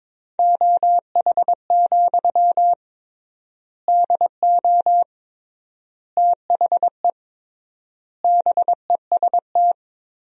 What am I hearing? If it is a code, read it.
Morse code, heard as a keyed tone, about 21 wpm, 700 Hz. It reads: OH, DO THE BEST
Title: 3x Word-Spacing - Sentences from Top 500 Words - 22wpm